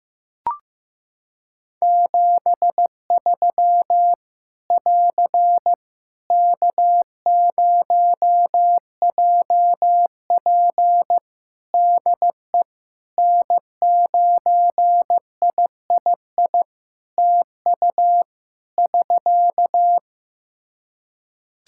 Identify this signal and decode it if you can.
Morse code, heard as a keyed tone, about 15 wpm, 700 Hz. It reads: 73 <AR> K0JP DE N9III TU <SK>
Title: Practice QSO - Exchange 4 - Ending - 15wpm